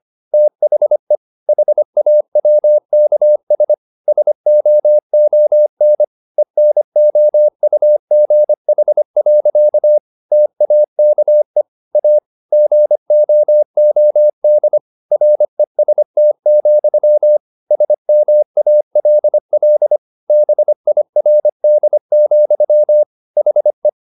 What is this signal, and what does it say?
Morse code, heard as a keyed tone, about 25 wpm, 600 Hz. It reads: THE HAWKS SOON ENOUGH. TAKE A GOOD REST, SMALL BIRD, HE
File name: the old man and the sea part 1 25wpm